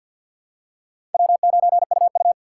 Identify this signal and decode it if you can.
Morse code, heard as a keyed tone, about 52 wpm, 700 Hz. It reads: W9UU
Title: USA Call Signs - 4 Characters - 6x Character Spacing - 50wpm